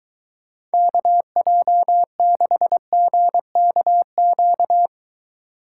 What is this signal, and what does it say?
Morse code, heard as a keyed tone, about 23 wpm, 700 Hz. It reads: KJ6GKQ